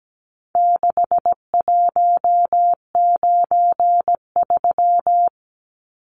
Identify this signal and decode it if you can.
Morse code, heard as a keyed tone, about 17 wpm, 700 Hz. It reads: 6193
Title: Numbers - 4-Digit 17wpm